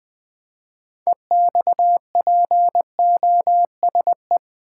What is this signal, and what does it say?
Morse code, heard as a keyed tone, about 20 wpm, 700 Hz. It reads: EXPOSE
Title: 6 Letter Words - First 3 Letters Given - 20wpm